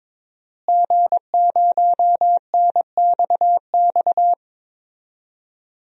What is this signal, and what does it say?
Morse code, heard as a keyed tone, about 22 wpm, 700 Hz. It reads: G0NXX
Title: DXCC-100 Call Signs - 5 Characters - Repeated 2x - 22wpm